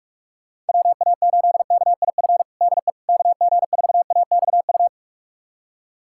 Morse code, heard as a keyed tone, about 46 wpm, 700 Hz. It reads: WA8KIF DE KG4AXU
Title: USA Callsign DE Callsign - 6 Characters - 45wpm